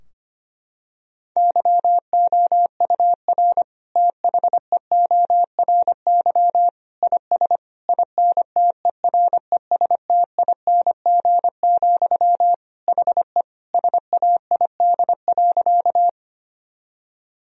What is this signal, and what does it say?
Morse code, heard as a keyed tone, about 25 wpm, 700 Hz. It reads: YOUR THEORY IS INTERESTING, HE SAID.
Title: Sentences from Top 1000 Words - Review Part 3 - 25wpm